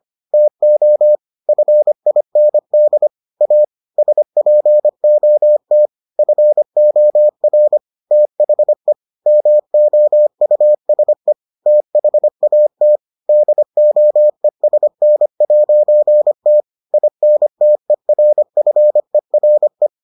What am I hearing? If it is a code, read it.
Morse code, heard as a keyed tone, about 25 wpm, 600 Hz. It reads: TO FIND A SPOT FOR THE MOUSE THAT DOESN'T INTERFERE